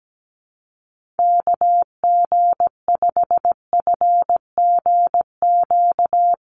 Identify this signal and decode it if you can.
Morse code, heard as a keyed tone, about 17 wpm, 700 Hz. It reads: KG5FGQ
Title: USA Call Signs Repeated 3x 17wpm